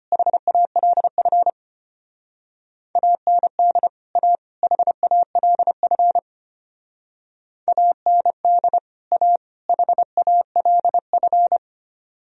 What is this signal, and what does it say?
Morse code, heard as a keyed tone, about 31 wpm, 700 Hz. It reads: HALF AND A HALF AND A HALF
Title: Speed-Racing - Sets of 3 Words 25wpm